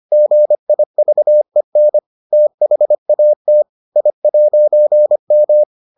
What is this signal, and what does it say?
Morse code, heard as a keyed tone, about 25 wpm, 600 Hz. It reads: GIVEN THAT I'M